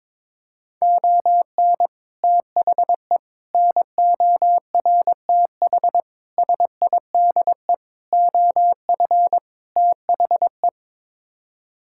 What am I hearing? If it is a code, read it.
Morse code, heard as a keyed tone, about 22 wpm, 700 Hz. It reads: ON THE NORTH SIDE OF THE
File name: practice / 22wpm / Sets of 6 Words - 22wpm